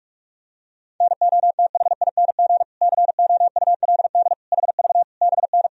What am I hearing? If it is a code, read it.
Morse code, heard as a keyed tone, about 46 wpm, 700 Hz. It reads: NOTHING COULD HV BN
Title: Sentences from Top 500 Words and Abbreviations 45wpm